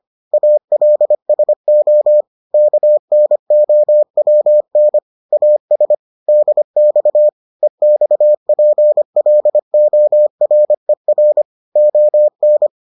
Morse code, heard as a keyed tone, about 25 wpm, 600 Hz. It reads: ALSO KNOWN AS DX EXPLORER ON